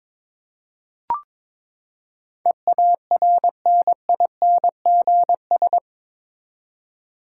Morse code, heard as a keyed tone, about 22 wpm, 700 Hz. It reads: EARNINGS
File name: practice / 22wpm / Eight Letter Words - 22wpm